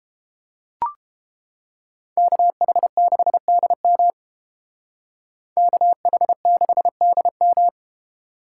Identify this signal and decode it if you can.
Morse code, heard as a keyed tone, about 30 wpm, 700 Hz. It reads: KH6DM KH6DM